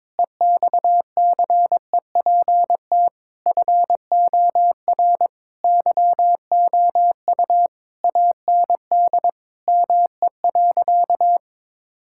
Morse code, heard as a keyed tone, about 22 wpm, 700 Hz. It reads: EXCEPT FOR YOU AND ME.
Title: Sentences from Top 1600 Words 22wpm